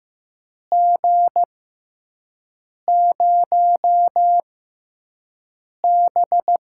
Morse code, heard as a keyed tone, about 15 wpm, spaced as for 5 wpm, 700 Hz. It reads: G0B
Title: USA Call Signs - 5 Characters - 6x Character Spacing - 15wpm